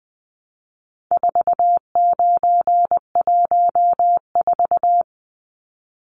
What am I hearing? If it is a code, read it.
Morse code, heard as a keyed tone, about 20 wpm, 700 Hz. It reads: 4914
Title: Numbers - 4-Digit 20wpm